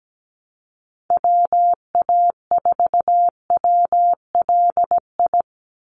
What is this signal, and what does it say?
Morse code, heard as a keyed tone, about 17 wpm, 700 Hz. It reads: WA4WLI
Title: Call Signs - 17wpm